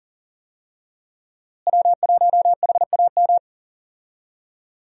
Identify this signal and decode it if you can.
Morse code, heard as a keyed tone, about 40 wpm, 700 Hz. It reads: W1HAM
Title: USA Call Signs - 5 Characters - Repeated 2x - 40wpm